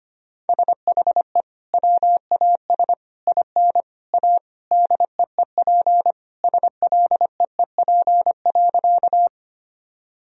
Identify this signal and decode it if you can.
Morse code, heard as a keyed tone, about 25 wpm, 700 Hz. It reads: SHE WAS IN A DEEP SLEEP.
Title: Sentences from Top 1000 Words - Review Part 3 - 25wpm